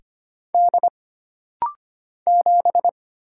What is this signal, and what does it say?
Morse code, heard as a keyed tone, about 25 wpm, 700 Hz. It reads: D 7
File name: Single Character (Letter:Number:Punctuation:Prosign) - Rapid-Fire - 25wpm